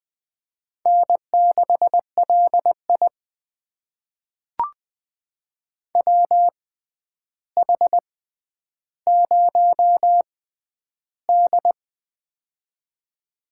Morse code, heard as a keyed tone, about 20 wpm, spaced as for 10 wpm, 700 Hz. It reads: N6LI WH0D
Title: USA Call Signs - 6x Character Spacing - 20wpm